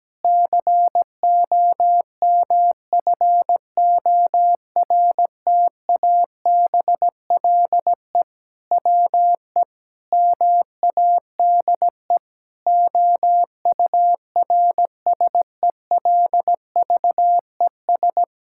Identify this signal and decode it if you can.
Morse code, heard as a keyed tone, about 17 wpm, 700 Hz. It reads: COMFORTABLE WE MADE OURSELVES
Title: Sentences from Top 1600 Words 17wpm